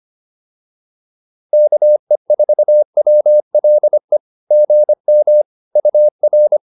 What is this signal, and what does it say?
Morse code, heard as a keyed tone, about 25 wpm, 590 Hz. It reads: KE4WLE GM UR
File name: POTA209a 25